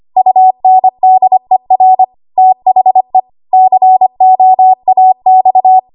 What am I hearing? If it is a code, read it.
Morse code, heard as a keyed tone, about 25 wpm, 750 Hz. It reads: UNDER THE COAX
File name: Archive / 25 WPM / 25WPM